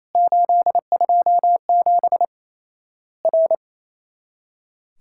Morse code, heard as a keyed tone, about 28 wpm, 700 Hz. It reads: 827 R
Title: Worked All Europe DX Contest - QTC Exchange - 28wpm